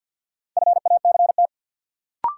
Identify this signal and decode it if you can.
Morse code, heard as a keyed tone, about 50 wpm, 700 Hz. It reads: FACT
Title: Sentences from Top 500 Words - New Word Drill - 50wpm